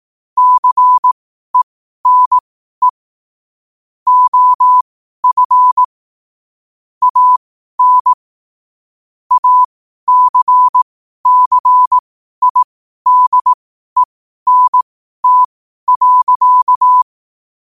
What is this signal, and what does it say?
Morse code, heard as a keyed tone, about 18 wpm, spaced as for 12 wpm, 1000 Hz. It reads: CENE OF AN ACCIDENT.